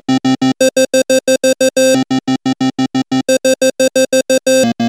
There is no code.